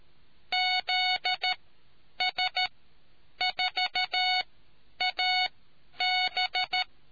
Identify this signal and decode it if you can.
Morse code, heard as a keyed tone, about 13 wpm, spaced as for 9 wpm, 740 Hz. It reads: ZS4AB